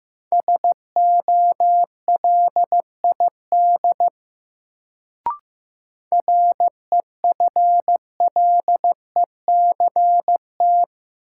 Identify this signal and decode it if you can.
Morse code, heard as a keyed tone, about 15 wpm, 700 Hz. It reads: SOLID REFLECT
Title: Sentences from Top 1500 Words - New Word Drill - 15wpm